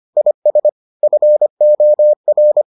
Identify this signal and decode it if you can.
Morse code, heard as a keyed tone, about 25 wpm, 600 Hz. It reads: IS FOR